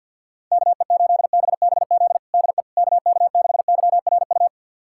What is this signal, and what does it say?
Morse code, heard as a keyed tone, about 51 wpm, 700 Hz. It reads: KE8BBZ DE KK6YRU
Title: USA Callsign DE Callsign - 6 Characters - 50wpm